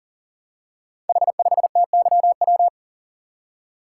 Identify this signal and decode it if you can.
Morse code, heard as a keyed tone, about 37 wpm, 700 Hz. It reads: H5TYW
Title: Code Groups - 5 Letter:Numbers - 40wpm